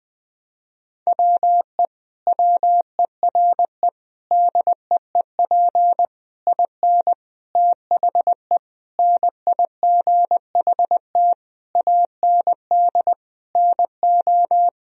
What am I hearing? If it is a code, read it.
Morse code, heard as a keyed tone, about 19 wpm, 700 Hz. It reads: WE WERE DEEP IN THE NIGHT AND NO
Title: Sentences from Top 1000 Words - Review Part 3 - 20wpm